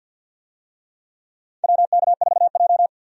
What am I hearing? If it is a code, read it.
Morse code, heard as a keyed tone, about 52 wpm, 700 Hz. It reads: WK4J
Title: USA Call Signs - 4 Characters - 50wpm